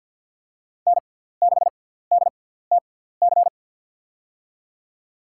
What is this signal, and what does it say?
Morse code, heard as a keyed tone, about 53 wpm, spaced as for 17 wpm, 700 Hz. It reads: N6DTC
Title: USA Call Signs - 6x Character Spacing - 50wpm